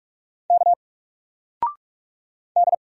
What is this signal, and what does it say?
Morse code, heard as a keyed tone, about 44 wpm, 700 Hz. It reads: K D
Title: Single Letter-Number 45wpm